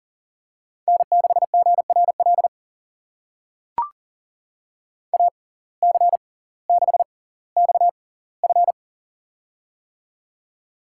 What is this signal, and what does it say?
Morse code, heard as a keyed tone, about 38 wpm, spaced as for 20 wpm, 700 Hz. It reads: N6GRL AC6XF
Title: USA Call Signs - 5 Characters - 6x Character Spacing - 40wpm